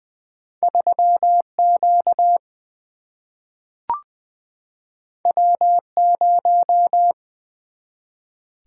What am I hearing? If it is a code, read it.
Morse code, heard as a keyed tone, about 20 wpm, 700 Hz. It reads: 3Q W0
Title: Single-Letter Single-Number 20wpm